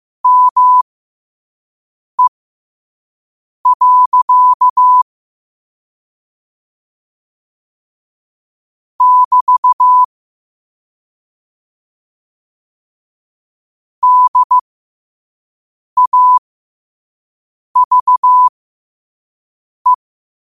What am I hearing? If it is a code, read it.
Morse code, heard as a keyed tone, about 15 wpm, spaced as for 5 wpm, 1000 Hz. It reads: ME. <BT> DAVE
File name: qotdfiles / 5wpm